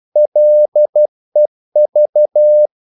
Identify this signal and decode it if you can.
Morse code, heard as a keyed tone, about 12 wpm, 600 Hz. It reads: LEV
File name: Korte ord DK 12wpm